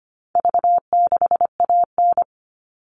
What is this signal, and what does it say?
Morse code, heard as a keyed tone, about 25 wpm, 700 Hz. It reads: V6AN
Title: All Cut Numbers - 4-Digit 25wpm